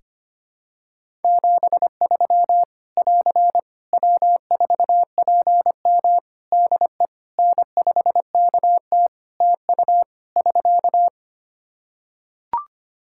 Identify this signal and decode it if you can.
Morse code, heard as a keyed tone, about 25 wpm, 700 Hz. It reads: 73 <AR> W4PM DE N5KT TU <SK>
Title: K1USN SST QSO Practice 25wpm